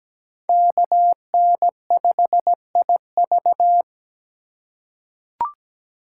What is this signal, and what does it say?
Morse code, heard as a keyed tone, about 17 wpm, 700 Hz. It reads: KN5IV
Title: Code Groups - 5 Letter:Numbers - 17wpm